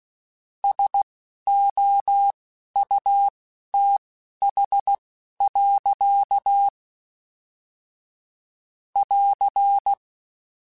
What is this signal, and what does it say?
Morse code, heard as a keyed tone, about 16 wpm, spaced as for 11 wpm, 800 Hz. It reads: SOUTH. <AR>
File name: Weather Sets / Weather Sets-mp3s / Weather 4 10 @10